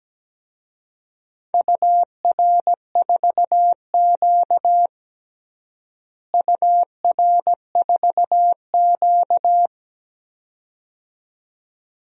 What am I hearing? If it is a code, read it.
Morse code, heard as a keyed tone, about 17 wpm, 700 Hz. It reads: UR4Q UR4Q
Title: DXCC-100 Call Signs - 4 Characters - Repeated 2x - 17wpm